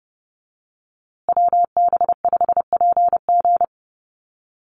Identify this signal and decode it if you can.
Morse code, heard as a keyed tone, about 30 wpm, 700 Hz. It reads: WB5PG
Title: USA Call Signs Repeated 2x 30wpm